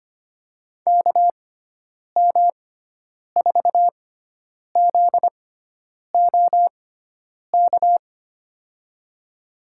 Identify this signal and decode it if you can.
Morse code, heard as a keyed tone, about 25 wpm, spaced as for 9 wpm, 700 Hz. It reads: KM4ZOK
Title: USA Call Signs - 6 Characters - 6x Character Spacing - 25wpm